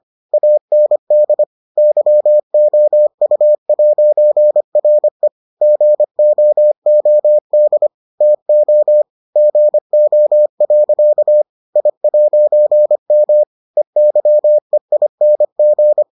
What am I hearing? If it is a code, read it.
Morse code, heard as a keyed tone, about 25 wpm, 600 Hz. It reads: AND YOU'RE GOOD TO GO. I'M EYEING